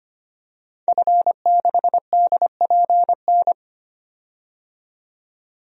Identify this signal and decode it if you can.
Morse code, heard as a keyed tone, about 25 wpm, 700 Hz. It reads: F6DPN
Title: DXCC-100 Call Signs - 5 Characters - 25wpm